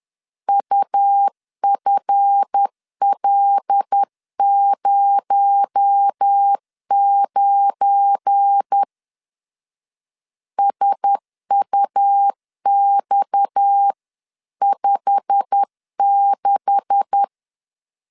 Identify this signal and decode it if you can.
Morse code, heard as a keyed tone, about 10 wpm, 800 Hz. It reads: UFL09 SUX56
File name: LEZIONI / 10 SIGNS / 10A SLASH 10wpm 2808